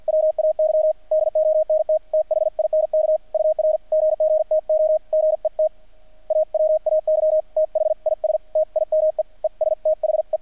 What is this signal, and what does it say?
Morse code, heard as a keyed tone, about 50 wpm, 620 Hz, with radio noise. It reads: WAY DONT THINK UR GG TO GET AWAY THIS TIME ES THE
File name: archived / CRSnet